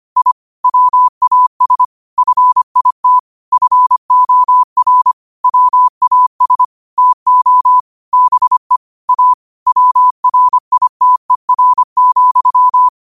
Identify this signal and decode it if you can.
Morse code, heard as a keyed tone, about 25 wpm, 1000 Hz. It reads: I WAS FIT FOR WAS TO BE A WRITER,
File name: qotdfiles / 25wpm